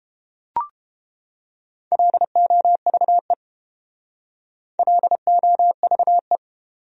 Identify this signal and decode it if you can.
Morse code, heard as a keyed tone, about 29 wpm, 700 Hz. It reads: LOVE LOVE